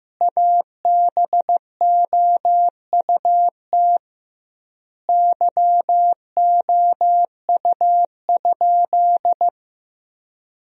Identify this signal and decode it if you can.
Morse code, heard as a keyed tone, about 15 wpm, 700 Hz. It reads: ABOUT YOU?
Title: 2x Word-Spacing - Sentences from Top 100 Words - 15wpm